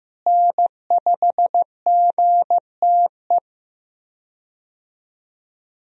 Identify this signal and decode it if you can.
Morse code, heard as a keyed tone, about 15 wpm, 700 Hz. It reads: N5GTE
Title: USA Call Signs - 5 Characters - Repeated 3x - 15wpm